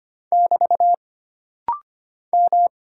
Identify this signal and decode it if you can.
Morse code, heard as a keyed tone, about 25 wpm, 700 Hz. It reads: <BT> M
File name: Single Character (Letter:Number:Punctuation:Prosign) - Rapid-Fire - 25wpm